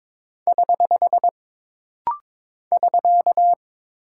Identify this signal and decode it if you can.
Morse code, heard as a keyed tone, about 22 wpm, 700 Hz. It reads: <HH> <SK>
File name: practice / 22wpm / Single Punctuation-Prosign - Rapid-Fire - 22wpm